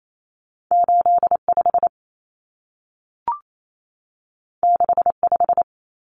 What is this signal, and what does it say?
Morse code, heard as a keyed tone, about 28 wpm, 700 Hz. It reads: 85 65
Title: Common Cut Numbers - 2-Digit 28wpm